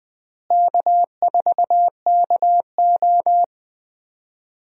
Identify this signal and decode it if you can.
Morse code, heard as a keyed tone, about 20 wpm, 700 Hz. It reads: K4KO
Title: USA Call Signs - 4 Characters - Repeated 2x - 20wpm